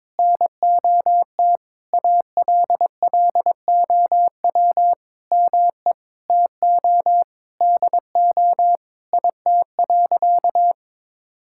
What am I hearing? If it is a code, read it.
Morse code, heard as a keyed tone, about 22 wpm, 700 Hz. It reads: NOT ALLOW ME TO DO IT.
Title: Sentences from Top 1000 Words - Review Part 5 - 22wpm